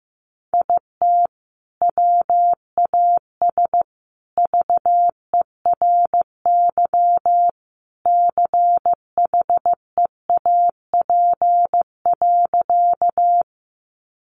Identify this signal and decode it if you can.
Morse code, heard as a keyed tone, about 15 wpm, 700 Hz. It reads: IT WAS VERY CHEAP.
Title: Sentences from Top 1200 Words 15wpm